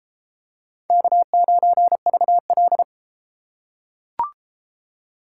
Morse code, heard as a keyed tone, about 32 wpm, 700 Hz. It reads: K9VL